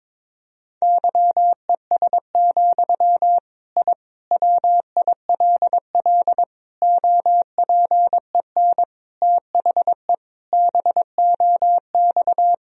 Morse code, heard as a keyed tone, about 22 wpm, 700 Hz. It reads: YES, I WILL OPEN THE BOX
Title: Sentences from Top 1000 Words - Review Part 2 - 22wpm